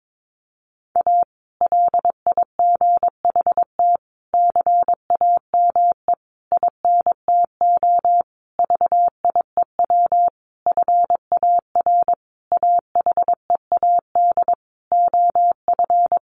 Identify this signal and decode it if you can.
Morse code, heard as a keyed tone, about 22 wpm, 700 Hz. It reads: A LIGHT CAME INTO VIEW FAR AHEAD OF
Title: Sentences from Top 1100 Words 22wpm